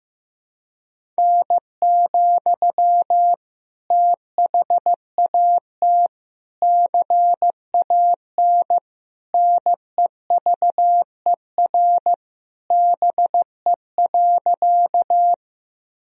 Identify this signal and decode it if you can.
Morse code, heard as a keyed tone, about 15 wpm, 700 Hz. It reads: N, THAT CAN NEVER BE.
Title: Sentences from Top 300 Words and Abbreviations 15wpm